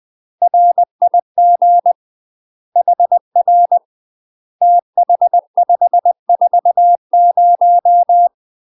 Morse code, heard as a keyed tone, about 20 wpm, 700 Hz. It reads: RIG HR TH540
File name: UK MASTER QSOs TRAINING/20WPM